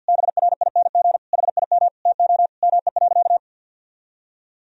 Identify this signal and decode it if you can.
Morse code, heard as a keyed tone, about 53 wpm, 700 Hz. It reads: BRING HIM TO ME.
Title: Sentences from Top 1000 Words - Review Part 3 - 50wpm